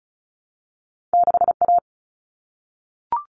Morse code, heard as a keyed tone, about 36 wpm, 700 Hz. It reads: 6A
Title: All Cut Numbers - 2-Digit 35wpm